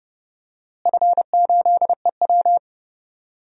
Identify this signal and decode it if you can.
Morse code, heard as a keyed tone, about 30 wpm, 700 Hz. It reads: F8EW
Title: DXCC-100 Call Signs - 4 Characters - Repeated 3x - 30wpm